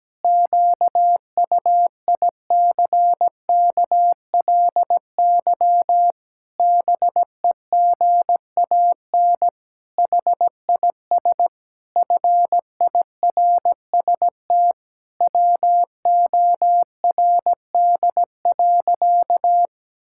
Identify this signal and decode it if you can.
Morse code, heard as a keyed tone, about 17 wpm, 700 Hz. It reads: QUICKLY BEGAN HIS FIRST WORD.
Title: Pangram Challenge v2 - 17wpm